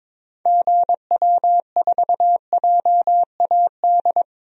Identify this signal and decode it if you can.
Morse code, heard as a keyed tone, about 23 wpm, 700 Hz. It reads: GW4JAD
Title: DXCC-100 Call Signs - 6 Characters - 22wpm